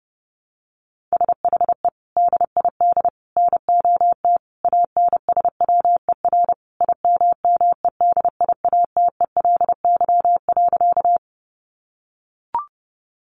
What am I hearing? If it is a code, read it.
Morse code, heard as a keyed tone, about 30 wpm, 700 Hz. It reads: SHE DID NOT ANSWER IMMEDIATELY.
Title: Sentences from Top 1100 Words 30wpm